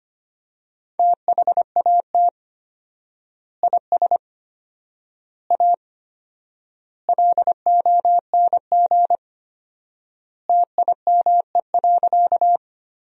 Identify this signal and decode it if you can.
Morse code, heard as a keyed tone, about 25 wpm, 700 Hz. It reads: THAT IS A LONG TIME.
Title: 4x Word-Spacing - Sentences from Top 100 Words - 25wpm